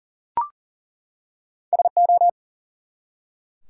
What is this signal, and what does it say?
Morse code, heard as a keyed tone, about 40 wpm, 700 Hz. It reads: SO
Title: Two Letter Words - 40wpm